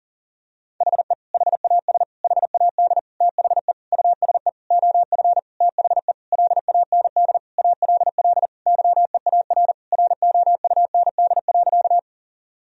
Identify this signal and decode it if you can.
Morse code, heard as a keyed tone, about 40 wpm, 700 Hz. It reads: HE HAS HAD THE USE OF THE LAND ALL YEAR ROUND.
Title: Sentences from Top 1000 Words - Review Part 1 - 40wpm